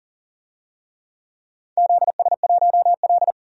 Morse code, heard as a keyed tone, about 38 wpm, 700 Hz. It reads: ZS1L
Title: DXCC-100 Call Signs - 4 Characters - 40wpm